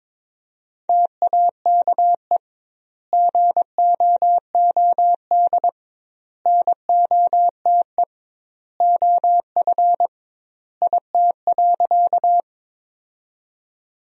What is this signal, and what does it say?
Morse code, heard as a keyed tone, about 22 wpm, 700 Hz. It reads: TAKE GOOD NOTE OF IT.
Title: 2x Word-Spacing - Sentences from Top 500 Words - 22wpm